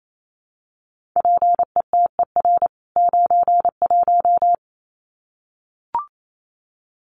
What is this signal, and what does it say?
Morse code, heard as a keyed tone, about 28 wpm, 700 Hz. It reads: PETER 91
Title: CWT Exchange 28wpm